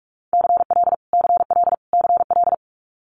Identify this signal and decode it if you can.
Morse code, heard as a keyed tone, about 46 wpm, 700 Hz. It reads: CL CL CL
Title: Multi-letter Phonemes Repeated 3x 45wpm